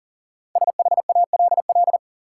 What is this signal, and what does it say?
Morse code, heard as a keyed tone, about 40 wpm, 700 Hz. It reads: SHALL
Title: Five Letter Words - 40wpm